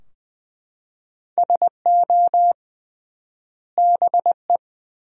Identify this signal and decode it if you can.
Morse code, heard as a keyed tone, about 20 wpm, 700 Hz. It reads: SO BE
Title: 3x Word-Spacing - Sentences from Top 100 Words - 20wpm